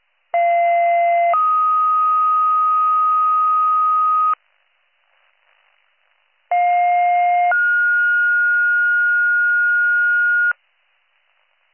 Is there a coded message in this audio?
Yes: two-tone page from 0.3 to 10.5 s, heard as a one-second tone, then a three-second tone, about 690 then 1250 Hz, about 710 then 1430 Hz.